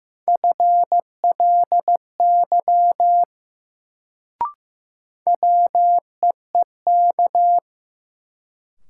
Morse code, heard as a keyed tone, about 15 wpm, 700 Hz. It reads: FLY WEEK
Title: Sentences from Top 500 Words - New Word Drill - 15wpm